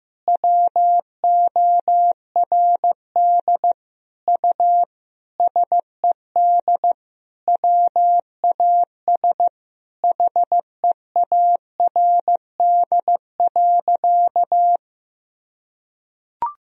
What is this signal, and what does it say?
Morse code, heard as a keyed tone, about 16 wpm, 700 Hz. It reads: WORD U SED WAS HEARD.
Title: Sentences from Top 400 Words and Abbreviations 15wpm